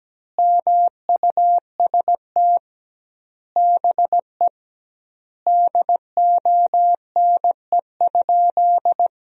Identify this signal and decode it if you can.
Morse code, heard as a keyed tone, about 17 wpm, 700 Hz. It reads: MUST BE DONE?
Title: 2x Word-Spacing - Sentences from Top 500 Words - 17wpm